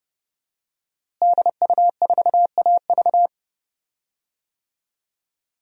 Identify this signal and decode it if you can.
Morse code, heard as a keyed tone, about 30 wpm, 700 Hz. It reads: DU4AV
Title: DXCC-100 Call Signs - 5 Characters - Repeated 2x - 30wpm